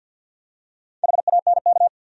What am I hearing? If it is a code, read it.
Morse code, heard as a keyed tone, about 48 wpm, 700 Hz. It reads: HANK
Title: USA Common Names - Max 4 Letters - 50wpm